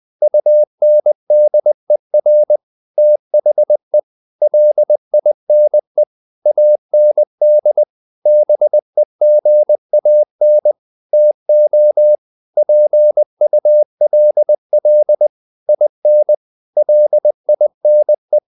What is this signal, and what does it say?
Morse code, heard as a keyed tone, about 20 wpm, 600 Hz. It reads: UNDER THE LINE AND BEGAN TO PULL IN LINE
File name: the old man and the sea part 2 20wpm